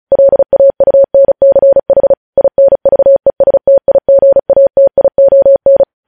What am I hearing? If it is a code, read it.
Morse code, heard as a keyed tone, about 36 wpm, 550 Hz. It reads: LAUNCH INVESTIGATION
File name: podcast / News headlines-1x-35x35